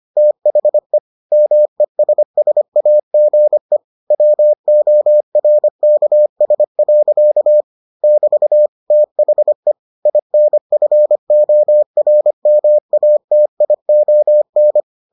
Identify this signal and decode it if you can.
Morse code, heard as a keyed tone, about 25 wpm, 600 Hz. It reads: THE MESSAGE WORKS. <BT> THE INFORMATION